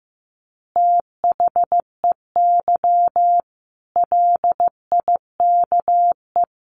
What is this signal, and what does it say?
Morse code, heard as a keyed tone, about 15 wpm, 700 Hz. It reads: THEY LIKE
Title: Easy Phrases 15wpm